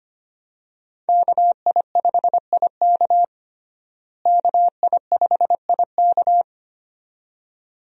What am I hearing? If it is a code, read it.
Morse code, heard as a keyed tone, about 25 wpm, 700 Hz. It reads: KI5IK KI5IK
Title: USA Call Signs - 5 Characters - Repeated 2x - 25wpm